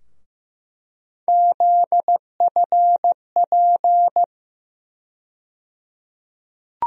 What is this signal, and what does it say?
Morse code, heard as a keyed tone, about 15 wpm, 700 Hz. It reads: ZFP
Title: USA Call Sign Postfixes - 3 Letters 15wpm